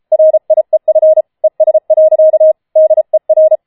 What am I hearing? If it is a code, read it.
Morse code, heard as a keyed tone, about 33 wpm, 630 Hz. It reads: RIEF ES. DER
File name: Schneewittchen Teil5 34wpm